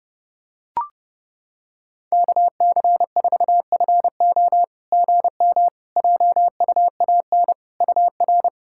Morse code, heard as a keyed tone, about 30 wpm, 700 Hz. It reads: KC4FO GM JUAN UR
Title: SOTA QSO Practice 30wpm